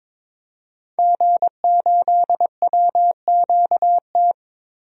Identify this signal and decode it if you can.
Morse code, heard as a keyed tone, about 22 wpm, 700 Hz. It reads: G8WQT
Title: DXCC-100 Call Signs - 5 Characters - Repeated 2x - 22wpm